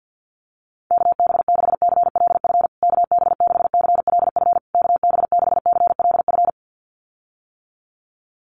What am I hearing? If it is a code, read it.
Morse code, heard as a keyed tone, about 50 wpm, 700 Hz. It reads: KB6CLF KB6CLF KB6CLF
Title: USA Call Signs Repeated 3x 50wpm